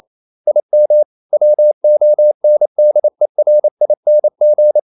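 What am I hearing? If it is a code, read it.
Morse code, heard as a keyed tone, about 28 wpm, 600 Hz. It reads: IM WONDERING
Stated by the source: Morse code practice at 28 wpm, text of Ken Follett's novel Never in English, from CW med Gnister.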